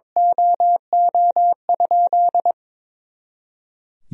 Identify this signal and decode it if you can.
Morse code, heard as a keyed tone, about 22 wpm, 700 Hz. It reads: OO?
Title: Sentences from Top 200 Words and Abbreviations 22wpm